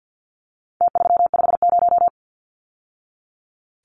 Morse code, heard as a keyed tone, about 49 wpm, 700 Hz. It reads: T350
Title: Numbers - Contest Style - 4-Digit 50wpm